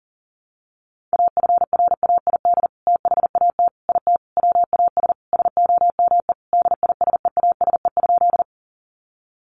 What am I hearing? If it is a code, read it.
Morse code, heard as a keyed tone, about 40 wpm, 700 Hz. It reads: AFRAID THAT IT WAS SOME DISEASE?
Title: Sentences from Top 1500 Words 40wpm